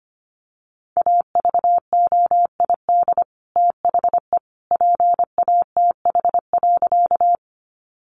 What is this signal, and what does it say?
Morse code, heard as a keyed tone, about 25 wpm, 700 Hz. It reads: AVOID THE PATH.
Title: Sentences from Top 1400 Words 25wpm